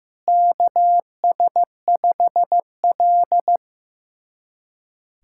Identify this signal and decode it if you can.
Morse code, heard as a keyed tone, about 15 wpm, 700 Hz. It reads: KS5L
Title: USA Callsign DE Callsign - 4 Characters - 15wpm